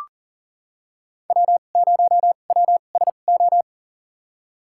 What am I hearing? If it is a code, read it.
Morse code, heard as a keyed tone, about 40 wpm, spaced as for 29 wpm, 700 Hz. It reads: W0WSO
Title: USA Call Signs - 5 Characters - 2x Character Spacing - 40wpm